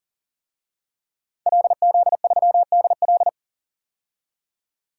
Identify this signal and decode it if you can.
Morse code, heard as a keyed tone, about 40 wpm, 700 Hz. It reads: LZ3DL
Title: DXCC-100 Call Signs - 5 Characters - 40wpm